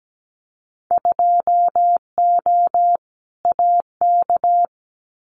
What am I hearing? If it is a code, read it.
Morse code, heard as a keyed tone, about 17 wpm, 700 Hz. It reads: 2O AK
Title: Winter Field Day Exchanges Encore 17wpm